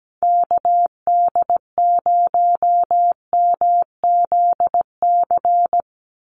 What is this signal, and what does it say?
Morse code, heard as a keyed tone, about 17 wpm, 700 Hz. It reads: KD0MZC
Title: USA Call Signs 17wpm